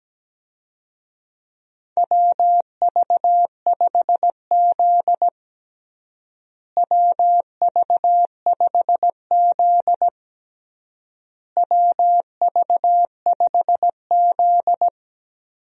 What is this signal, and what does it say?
Morse code, heard as a keyed tone, about 17 wpm, 700 Hz. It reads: WV5Z WV5Z WV5Z
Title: USA Call Signs - 4 Characters - Repeated 3x - 17wpm